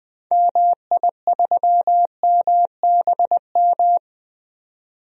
Morse code, heard as a keyed tone, about 20 wpm, 700 Hz. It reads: MI3MBM
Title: DXCC-100 Call Signs - 6 Characters - 20wpm